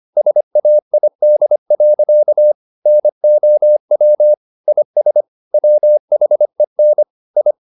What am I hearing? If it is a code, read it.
Morse code, heard as a keyed tone, about 25 wpm, 600 Hz. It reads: SAID. NOW IS WHEN I
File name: the old man and the sea part 1 25wpm